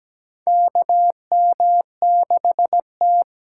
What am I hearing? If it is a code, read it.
Morse code, heard as a keyed tone, about 17 wpm, 700 Hz. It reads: KM6T
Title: USA Call Signs - 6 Characters - 8x Character Spacing - 17wpm